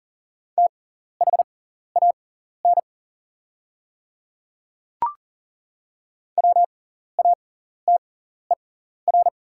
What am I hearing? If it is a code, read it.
Morse code, heard as a keyed tone, about 40 wpm, spaced as for 14 wpm, 690 Hz. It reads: THAN WATER
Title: 6x Character Spacing - Top 100 Words - 40wpm